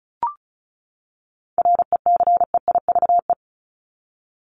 Morse code, heard as a keyed tone, about 34 wpm, 700 Hz. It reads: RECEIVE
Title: ING Suffix - Top 100 Words with Base Words 35wpm